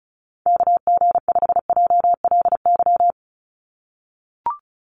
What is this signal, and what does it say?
Morse code, heard as a keyed tone, about 36 wpm, 700 Hz. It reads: KG5JLY
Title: USA Call Signs 35wpm